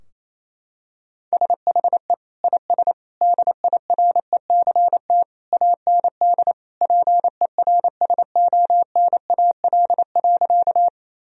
Morse code, heard as a keyed tone, about 28 wpm, 700 Hz. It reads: SHE IS DIRECT AND PERSONAL.
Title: Sentences from Top 1000 Words - Review Part 4 - 28wpm